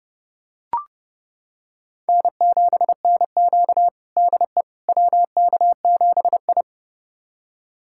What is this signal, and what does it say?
Morse code, heard as a keyed tone, about 32 wpm, 700 Hz. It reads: N7NQ DE WK7I
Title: USA Callsign DE Callsign - 4 Characters - 30wpm